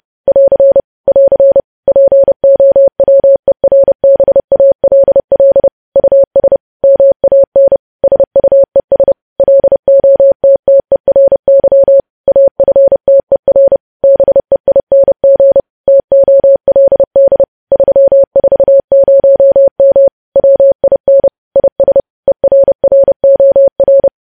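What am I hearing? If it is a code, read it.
Morse code, heard as a keyed tone, about 30 wpm, 550 Hz. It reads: <AR> <AR> POWERBALL US MAN SUES LOTTERY AFTER BEING TOLD 340M WIN IS ERROR
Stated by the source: Morse code news headlines practice at 30 WPM.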